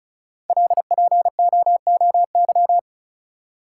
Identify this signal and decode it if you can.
Morse code, heard as a keyed tone, about 33 wpm, 700 Hz. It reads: LPOOY
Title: Code Groups - 5 Letters - 35wpm